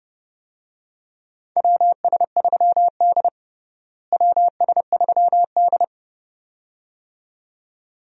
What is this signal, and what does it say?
Morse code, heard as a keyed tone, about 30 wpm, 700 Hz. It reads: WS3D WS3D
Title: USA Call Signs - 4 Characters - Repeated 2x - 30wpm